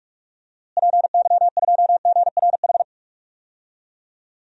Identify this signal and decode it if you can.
Morse code, heard as a keyed tone, about 46 wpm, 700 Hz. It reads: PY2GRH
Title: DXCC-100 Call Signs - 6 Characters - Repeated 2x - 45wpm